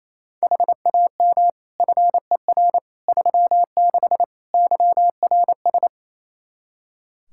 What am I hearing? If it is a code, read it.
Morse code, heard as a keyed tone, about 28 wpm, 700 Hz. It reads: HAM FER 36 YRS
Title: Practice QSO - Exchange 3 - More Info - 28wpm